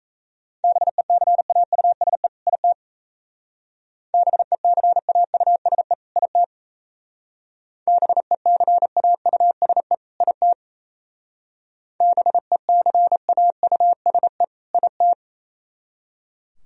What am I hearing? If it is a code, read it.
Morse code, heard as a keyed tone, about 35 wpm, 700 Hz. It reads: BECAUSE IT BECAUSE IT BECAUSE IT BECAUSE IT
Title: Speed-Racing - Sets of 2 Words 28wpm